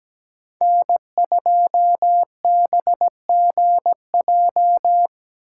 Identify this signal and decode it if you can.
Morse code, heard as a keyed tone, about 19 wpm, 700 Hz. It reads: N2BGJ
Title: Speed Racing - USA 1x3 Call Signs 17wpm